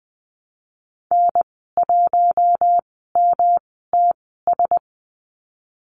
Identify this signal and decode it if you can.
Morse code, heard as a keyed tone, about 20 wpm, spaced as for 14 wpm, 700 Hz. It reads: N1MTS
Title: 122 Lesson 20g - Number 3 - Call Signs 20wpm